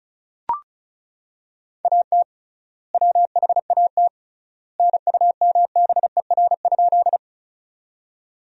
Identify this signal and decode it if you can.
Morse code, heard as a keyed tone, about 36 wpm, 700 Hz. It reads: AT WHAT NUMBER?
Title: 3x Word-Spacing - Sentences from Top 100 Words - 35wpm